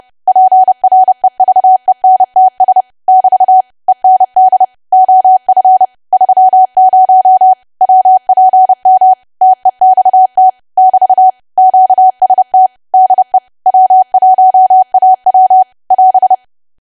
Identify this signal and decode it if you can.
Morse code, heard as a keyed tone, about 30 wpm, 750 Hz. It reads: PREVENTS <BT> END OF 30 WPM TEXT <BT> QST DE W1AW <AS>
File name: cwa / W1AW-Level3 / 30WPM